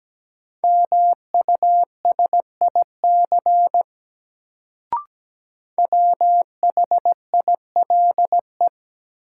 Morse code, heard as a keyed tone, about 17 wpm, 700 Hz. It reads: MUSIC WHILE
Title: Sentences from Top 300 Words - New Word Drill - 17wpm